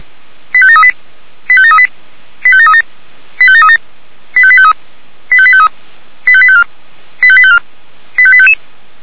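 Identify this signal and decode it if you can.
Five-tone selective call (ZVEI), heard as quick runs of five tones, tones of about 70 ms. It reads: 85738 85738 85737 85737 85853 85853 85854 85854 85855
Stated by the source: ZVEI five-tone selective call 85745, received on an SDR.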